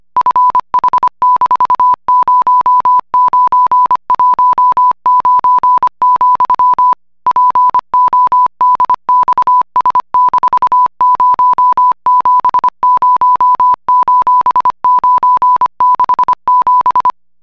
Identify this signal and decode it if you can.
Morse code, heard as a keyed tone, about 25 wpm, 1000 Hz. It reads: FH-0919, PODXS-070:967